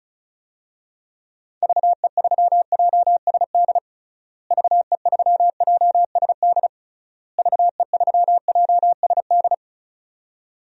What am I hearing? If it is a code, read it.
Morse code, heard as a keyed tone, about 36 wpm, 700 Hz. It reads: VE3JSD VE3JSD VE3JSD
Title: DXCC-100 Call Signs - 6 Characters - Repeated 3x - 35wpm